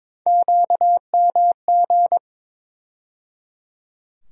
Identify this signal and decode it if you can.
Morse code, heard as a keyed tone, about 22 wpm, 700 Hz. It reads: QMG
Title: USA Call Sign Postfixes - 3 Letters 22wpm